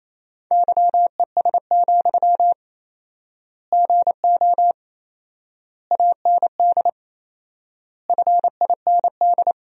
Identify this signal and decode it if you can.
Morse code, heard as a keyed tone, about 28 wpm, 700 Hz. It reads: YES, GO AND FIND
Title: 4x Word-Spacing - Sentences from Top 500 Words - 28wpm